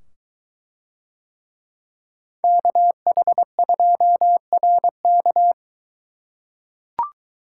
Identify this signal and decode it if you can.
Morse code, heard as a keyed tone, about 22 wpm, 700 Hz. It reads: KH2RK